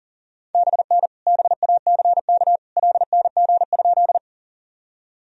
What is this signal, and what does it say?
Morse code, heard as a keyed tone, about 40 wpm, 700 Hz. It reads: BN BACK LNG?
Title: Sentences from Top 200 Words and Abbreviations 40wpm